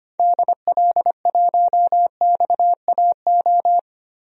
Morse code, heard as a keyed tone, about 25 wpm, 700 Hz. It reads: DL1XAO
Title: DXCC-100 Call Signs - 6 Characters - Repeated 2x - 25wpm